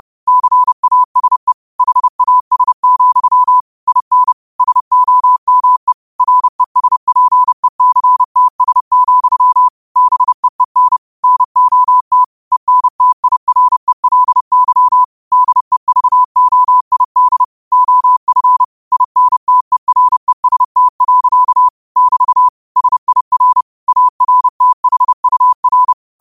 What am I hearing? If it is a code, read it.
Morse code, heard as a keyed tone, about 30 wpm, 1000 Hz. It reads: CASE HAS, IN SOME RESPECTS, BEEN NOT ENTIRELY DEVOID OF INTEREST. <BT> SIR ARTHUR
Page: Quotes for Wed, 13 Aug 2025 in Morse Code at 30 words per minute.